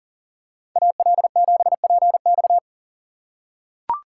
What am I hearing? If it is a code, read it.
Morse code, heard as a keyed tone, about 40 wpm, 690 Hz. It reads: AL7PX
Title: USA Call Signs - 5 Characters - 3x Character Spacing - 40wpm